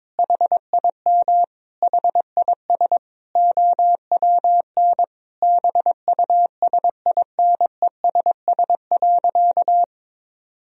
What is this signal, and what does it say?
Morse code, heard as a keyed tone, about 22 wpm, 700 Hz. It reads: HIM HIS OWN BUSINESS.
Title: Sentences from Top 1000 Words - Review Part 2 - 22wpm